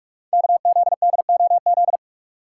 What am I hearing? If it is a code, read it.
Morse code, heard as a keyed tone, about 44 wpm, 700 Hz. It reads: KZDOZ
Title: Code Groups - 5 Letter:Numbers - 45wpm